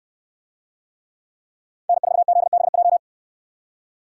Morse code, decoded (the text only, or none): N4ZBJ